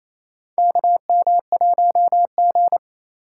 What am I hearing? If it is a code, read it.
Morse code, heard as a keyed tone, about 28 wpm, 700 Hz. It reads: KM1G
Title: USA Call Signs - 2x Character Spacing - 28wpm